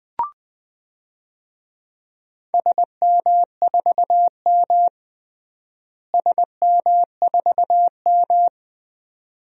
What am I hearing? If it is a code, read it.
Morse code, heard as a keyed tone, about 20 wpm, 700 Hz. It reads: SM4M SM4M
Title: DXCC-100 Call Signs - 4 Characters - Repeated 2x - 20wpm